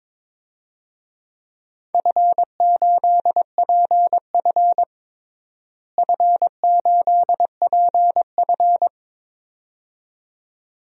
Morse code, heard as a keyed tone, about 21 wpm, 700 Hz. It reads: F8PF F8PF
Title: DXCC-100 Call Signs - 4 Characters - Repeated 2x - 22wpm